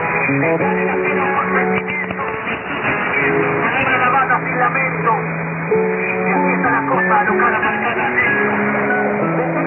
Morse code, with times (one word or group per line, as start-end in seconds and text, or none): none